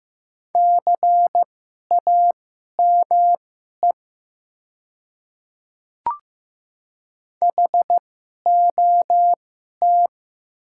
Morse code, heard as a keyed tone, about 15 wpm, spaced as for 11 wpm, 700 Hz. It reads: CAME HOT
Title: 2x Character Spacing - Top 300 Words - 15wpm